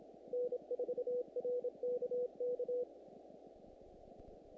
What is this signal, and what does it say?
Morse code, heard as a keyed tone, about 27 wpm, 490 Hz. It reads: N4RKK
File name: contests / naqp-cw-k5tr-op-k5pi- -1900